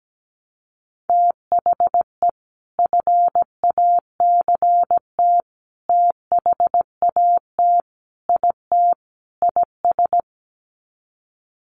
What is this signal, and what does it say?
Morse code, heard as a keyed tone, about 17 wpm, 700 Hz. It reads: THE FACT THAT IT IS
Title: Sets of 5 Words Encore2 17wpm